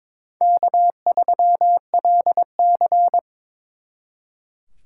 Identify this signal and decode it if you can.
Morse code, heard as a keyed tone, about 26 wpm, 700 Hz. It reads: K3LC
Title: Speed Racing - USA 1x2 Call Signs 22wpm